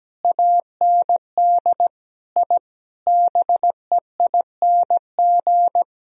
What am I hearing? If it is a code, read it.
Morse code, heard as a keyed tone, about 17 wpm, 700 Hz. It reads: AND I BEING
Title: Sentences from Top 1000 Words - Review Part 3 - 17wpm